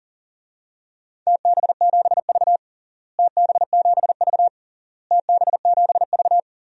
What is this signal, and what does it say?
Morse code, heard as a keyed tone, about 40 wpm, 700 Hz. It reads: TB7V TB7V TB7V
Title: DXCC-100 Call Signs - 4 Characters - Repeated 3x - 40wpm